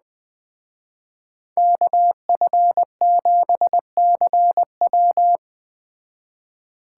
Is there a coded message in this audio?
Yes: Morse code from 1.6 to 5.4 s, heard as a keyed tone, about 21 wpm, 700 Hz. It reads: KF7CW